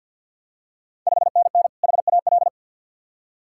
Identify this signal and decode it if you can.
Morse code, heard as a keyed tone, about 52 wpm, 690 Hz. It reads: 5NN HAL
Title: Ontario QSO Party Exchange 50wpm